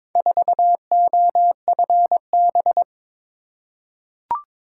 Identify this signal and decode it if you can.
Morse code, heard as a keyed tone, about 22 wpm, 700 Hz. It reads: 4OFB
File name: practice / 22wpm / Four Letter-Number Combinations - 22wpm